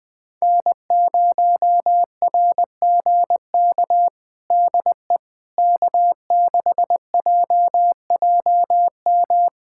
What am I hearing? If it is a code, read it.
Morse code, heard as a keyed tone, about 20 wpm, 700 Hz. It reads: N0RGK DE K6JJM
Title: USA Callsign DE Callsign - 5 Characters - 20wpm